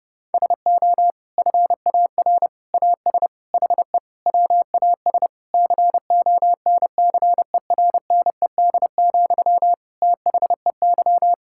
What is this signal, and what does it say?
Morse code, heard as a keyed tone, about 32 wpm, 700 Hz. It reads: SO FAR AS HE WAS CONCERNED, THEY
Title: Sentences from Top 1000 Words - Review Part 6 - 30wpm